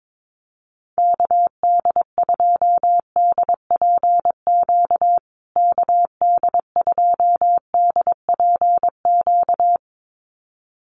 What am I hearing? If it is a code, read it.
Morse code, heard as a keyed tone, about 22 wpm, 700 Hz. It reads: KD2DPQ KD2DPQ
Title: USA Call Signs Repeated 2x 22wpm